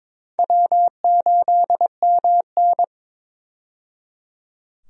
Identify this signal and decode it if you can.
Morse code, heard as a keyed tone, about 22 wpm, 700 Hz. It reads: W8MN
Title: USA Call Signs - 4 Characters - 22wpm